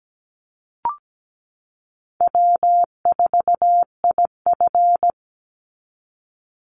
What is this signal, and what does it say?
Morse code, heard as a keyed tone, about 17 wpm, 700 Hz. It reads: W4IF
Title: Easy Call Signs Repeated 3x - 17wpm